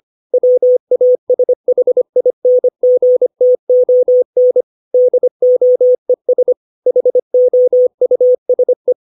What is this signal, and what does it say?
Morse code, heard as a keyed tone, about 25 wpm, 500 Hz. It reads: WASHINGTON DOES HOUSE